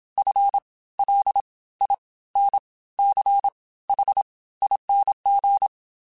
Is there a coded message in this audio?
Yes: Morse code from 0.2 to 5.7 s, heard as a keyed tone, about 27 wpm, spaced as for 17 wpm, 800 Hz. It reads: FLINCHING